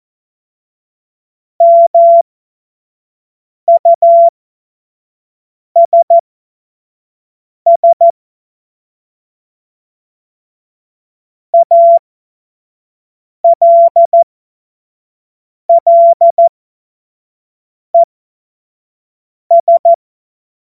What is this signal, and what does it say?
Morse code, heard as a keyed tone, about 14 wpm, spaced as for 5 wpm, 680 Hz. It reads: MUSS ALLES
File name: Morsekurs / Witze / Witz 09. Dezember 2025 25 F70